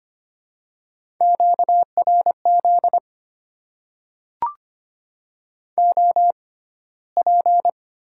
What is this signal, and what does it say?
Morse code, heard as a keyed tone, about 25 wpm, spaced as for 15 wpm, 690 Hz. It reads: QRZ OP
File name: Top 90 QSO Elements - 6x Character Spacing - 25wpm